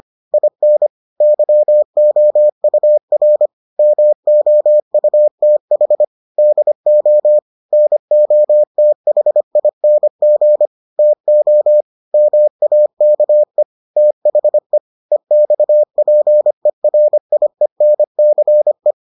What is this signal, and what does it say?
Morse code, heard as a keyed tone, about 25 wpm, 600 Hz. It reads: IN YOUR MOUTH DO NOTHING TO MAKE THE EXPERIENCE